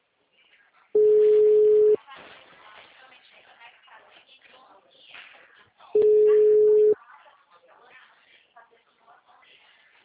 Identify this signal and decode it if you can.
Ringback tone (Europe, 425 Hz), with speech.